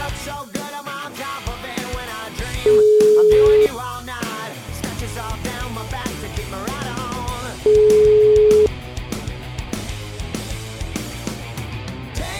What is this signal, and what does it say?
Ringback tone (Europe, 425 Hz).